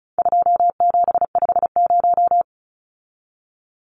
Morse code, heard as a keyed tone, about 34 wpm, 700 Hz. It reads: 2750
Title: Numbers - 4-Digit 35wpm